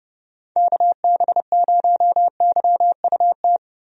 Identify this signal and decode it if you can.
Morse code, heard as a keyed tone, about 29 wpm, 700 Hz. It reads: KB0YUT
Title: USA Call Signs - 6 Characters - Repeated 2x - 30wpm